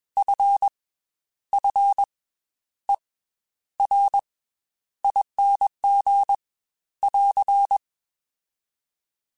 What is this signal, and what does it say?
Morse code, heard as a keyed tone, about 21 wpm, 800 Hz. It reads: F F E R ING <AR>